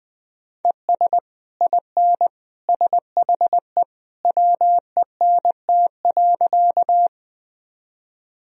Morse code, heard as a keyed tone, about 20 wpm, 700 Hz. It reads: ES IN SHE WENT.
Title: Sentences from Top 200 Words and Abbreviations 20wpm